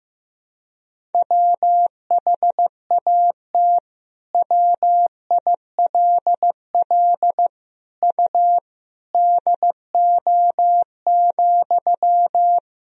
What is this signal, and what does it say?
Morse code, heard as a keyed tone, about 15 wpm, 700 Hz. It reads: WHAT WILL U DO,
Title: Sentences from Top 100 Words and Abbreviations 15wpm